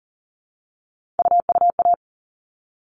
Morse code, heard as a keyed tone, about 40 wpm, 700 Hz. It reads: UUA
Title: All Cut Numbers - 3-Digit 40wpm